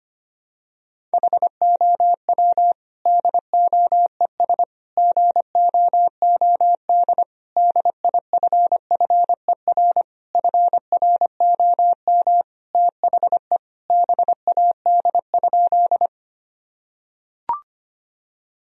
Morse code, heard as a keyed tone, about 25 wpm, 700 Hz. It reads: HOW DOES GOOD DIFFER FROM THE BAD?
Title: Sentences from Top 1000 Words - Review Part 1 - 25wpm